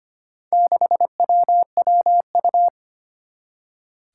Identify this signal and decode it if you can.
Morse code, heard as a keyed tone, about 25 wpm, 700 Hz. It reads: 6WWU
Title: Four Challenging Letter-Number Combinations 25wpm